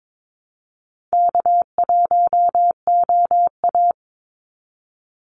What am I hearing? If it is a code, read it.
Morse code, heard as a keyed tone, about 22 wpm, 700 Hz. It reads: K1OA
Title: Easy Call Signs Repeated 3x - 22wpm